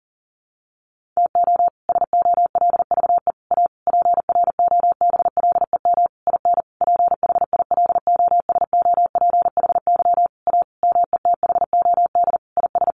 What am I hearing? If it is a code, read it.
Morse code, heard as a keyed tone, about 40 wpm, 700 Hz. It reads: TO SOLVE A PROBLEM IN PHILOSOPHY A METHOD IS
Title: Sentences from Top 1400 Words 40wpm